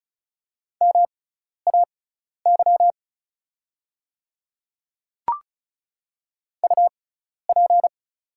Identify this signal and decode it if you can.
Morse code, heard as a keyed tone, about 36 wpm, spaced as for 12 wpm, 700 Hz. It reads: MAY UP
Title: 6x Character Spacing - Top 100 Words - 35wpm